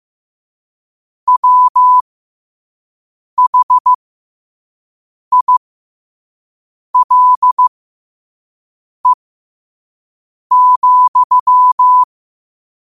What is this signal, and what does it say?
Morse code, heard as a keyed tone, about 15 wpm, spaced as for 5 wpm, 1000 Hz. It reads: WHILE,
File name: qotdfiles / 5wpm